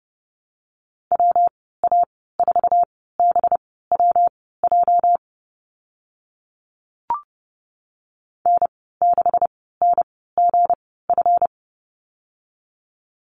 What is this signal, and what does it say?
Morse code, heard as a keyed tone, about 30 wpm, spaced as for 17 wpm, 700 Hz. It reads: WA4BWJ N6NGF
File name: Lesson 37g - Letter x - Call Signs 30wpm